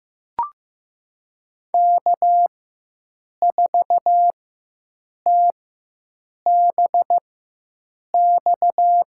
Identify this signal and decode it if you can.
Morse code, heard as a keyed tone, about 15 wpm, spaced as for 7 wpm, 700 Hz. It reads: K4TBX
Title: USA Call Signs - 4x Character Spacing - 15wpm